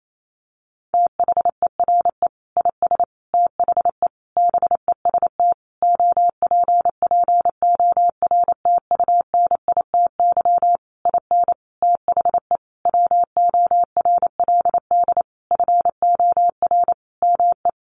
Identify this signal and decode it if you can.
Morse code, heard as a keyed tone, about 28 wpm, 700 Hz. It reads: THERE IS THE BEST OPPORTUNITY IN THE WORLD FOR ME
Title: Sentences from Top 1500 Words 28wpm